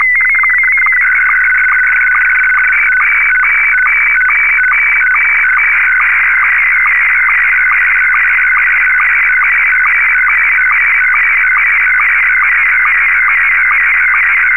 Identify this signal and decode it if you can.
SSTV (slow-scan television), Scottie 1.